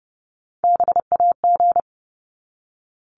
Morse code, heard as a keyed tone, about 30 wpm, 700 Hz. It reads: BAG
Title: Top 1501-1750 Words 30wpm